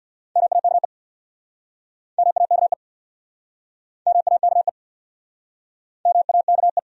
Morse code, heard as a keyed tone, about 68 wpm, 700 Hz. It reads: MAKE MAKE MAKE MAKE